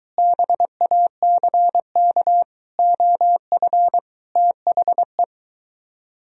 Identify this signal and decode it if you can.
Morse code, heard as a keyed tone, about 23 wpm, 700 Hz. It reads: BACK OF THE